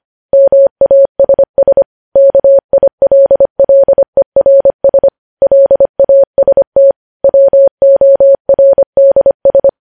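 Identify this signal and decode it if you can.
Morse code, heard as a keyed tone, about 25 wpm, 550 Hz. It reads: MASS KILLERS LAST WORDS
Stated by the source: News headlines sent in Morse code at 25 wpm.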